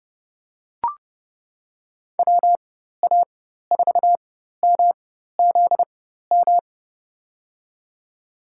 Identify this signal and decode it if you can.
Morse code, heard as a keyed tone, about 32 wpm, spaced as for 14 wpm, 700 Hz. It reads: WA4MZM
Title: USA Call Signs - 6 Characters - 4x Character Spacing - 30wpm